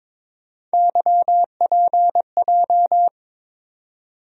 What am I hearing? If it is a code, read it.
Morse code, heard as a keyed tone, about 21 wpm, 700 Hz. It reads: YPJ
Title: USA Call Sign Postfixes - 3 Letters 22wpm